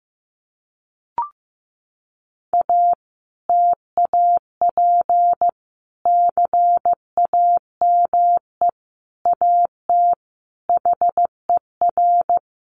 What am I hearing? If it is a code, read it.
Morse code, heard as a keyed tone, about 15 wpm, 700 Hz. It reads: A TAP CAME AT HER
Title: Sentences from Top 1400 Words 15wpm